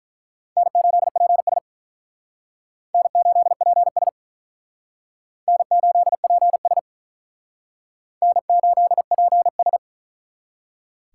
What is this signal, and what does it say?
Morse code, heard as a keyed tone, about 43 wpm, 700 Hz. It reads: N8PS N8PS N8PS N8PS
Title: Speed Racing - USA 1x2 Call Signs 35wpm